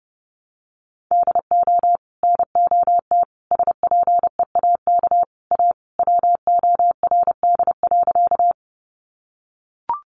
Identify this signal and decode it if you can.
Morse code, heard as a keyed tone, about 30 wpm, 700 Hz. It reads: DO NOT SPEAK A WORD.
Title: Sentences from Top 1100 Words 30wpm